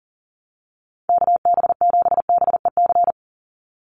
Morse code, heard as a keyed tone, about 40 wpm, 700 Hz. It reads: KB7BEC
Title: USA Call Signs 40wpm